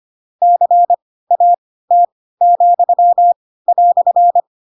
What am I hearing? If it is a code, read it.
Morse code, heard as a keyed tone, about 25 wpm, spaced as for 16 wpm, 700 Hz. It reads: CAT,"
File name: Pinocchio / Pinocchio ch7 10 25 ik1hge